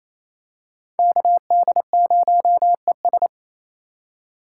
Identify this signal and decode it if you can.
Morse code, heard as a keyed tone, about 27 wpm, 700 Hz. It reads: KD0ES
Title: USA Call Signs - 4x Character Spacing - 28wpm